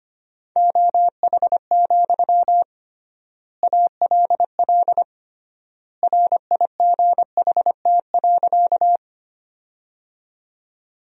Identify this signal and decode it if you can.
Morse code, heard as a keyed tone, about 25 wpm, 700 Hz. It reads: OH, ALL RIGHT.
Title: 3x Word-Spacing - Sentences from Top 500 Words - 25wpm